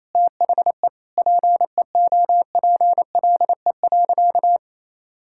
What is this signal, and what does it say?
Morse code, heard as a keyed tone, about 28 wpm, 700 Hz. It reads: THE PEOPLE.
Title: Sentences from Top 1600 Words 28wpm